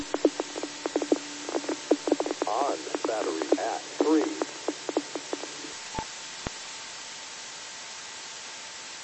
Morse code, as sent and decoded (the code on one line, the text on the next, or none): none